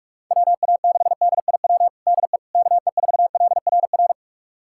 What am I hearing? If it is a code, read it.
Morse code, heard as a keyed tone, about 46 wpm, 700 Hz. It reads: WA6DIW DE KE4LRR
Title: USA Callsign DE Callsign 45wpm